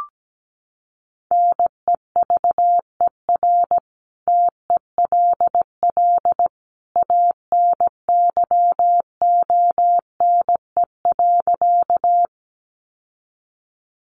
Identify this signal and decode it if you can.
Morse code, heard as a keyed tone, about 17 wpm, 700 Hz. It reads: NEVER TELL ANYONE.
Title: Sentences from Top 900 Words 17wpm